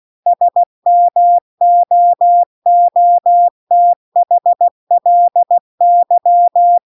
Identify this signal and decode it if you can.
Morse code, heard as a keyed tone, about 16 wpm, 700 Hz. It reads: SMOOTHLY
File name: CW-The-Big-Twoheaded-River-2-16WPM